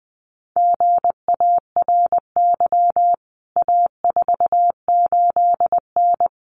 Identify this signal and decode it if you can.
Morse code, heard as a keyed tone, about 20 wpm, 700 Hz. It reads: GARY A48N
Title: CWT Exchange 20wpm